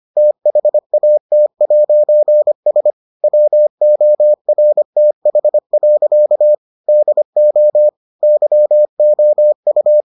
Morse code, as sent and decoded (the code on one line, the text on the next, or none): - .... .- - .----. ... / .-- --- .-. - .... .-.-.- / -.. --- / -.-- --- ..-
THAT'S WORTH. DO YOU